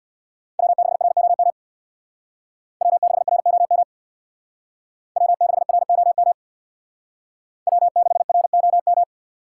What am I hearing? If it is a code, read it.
Morse code, heard as a keyed tone, about 62 wpm, 700 Hz. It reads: W6ROM W6ROM W6ROM W6ROM
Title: Speed Racing - USA 1x3 Call Signs 50wpm